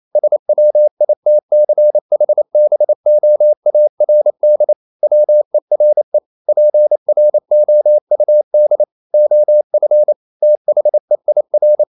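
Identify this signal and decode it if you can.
Morse code, heard as a keyed tone, about 28 wpm, 600 Hz. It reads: SWITCHBOARD WERE PROUD OF THEIR
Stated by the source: Morse code practice at 28 wpm, text of Ken Follett's novel Never in English, from CW med Gnister.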